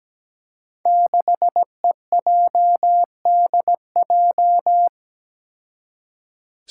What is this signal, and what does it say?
Morse code, heard as a keyed tone, about 17 wpm, 700 Hz. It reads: 6EJDJ
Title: Five Challenging Letter-Number Combinations 17wpm